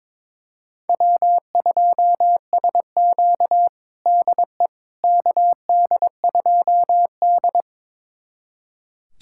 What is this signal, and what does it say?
Morse code, heard as a keyed tone, about 22 wpm, 700 Hz. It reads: W2SQ DE KD2D
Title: USA Callsign DE Callsign - 4 Characters - 22wpm